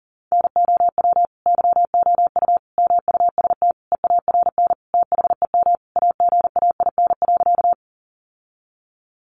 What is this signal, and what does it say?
Morse code, heard as a keyed tone, about 40 wpm, 700 Hz. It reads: NOW YOU MUST EARN THEM AGAIN.
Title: Sentences from Top 1500 Words 40wpm